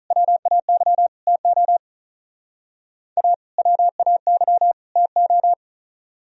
Morse code, heard as a keyed tone, about 44 wpm, 700 Hz. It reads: WAY TO A WAY TO